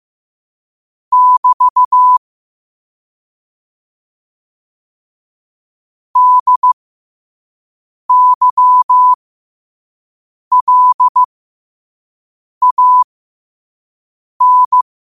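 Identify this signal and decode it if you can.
Morse code, heard as a keyed tone, about 15 wpm, spaced as for 5 wpm, 1000 Hz. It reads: <BT> DYLAN
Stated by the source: Morse code quote of the day at 5 wpm.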